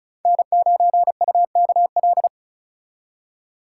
Morse code, heard as a keyed tone, about 43 wpm, 700 Hz. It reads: N9UKL
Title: Speed Racing - USA 1x3 Call Signs 35wpm